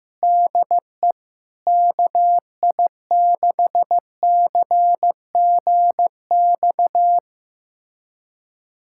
Morse code, heard as a keyed tone, about 15 wpm, 700 Hz. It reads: DE KI6CGX
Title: USA Callsign DE Callsign 15wpm